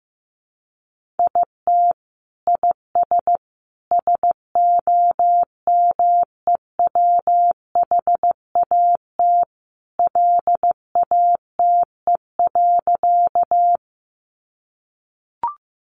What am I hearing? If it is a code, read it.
Morse code, heard as a keyed tone, about 15 wpm, 700 Hz. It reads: IT IS SOMEWHAT LATE.
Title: Sentences from Top 1300 Words 15wpm